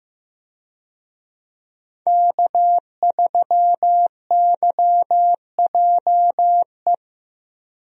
Morse code, heard as a keyed tone, about 15 wpm, 700 Hz. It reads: K3YJE
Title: USA Call Signs - 5 Characters - 15wpm